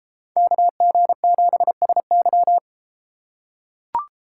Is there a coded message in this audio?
Yes: Morse code from 0.4 to 2.6 s, heard as a keyed tone, about 31 wpm, 700 Hz. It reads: KG7SY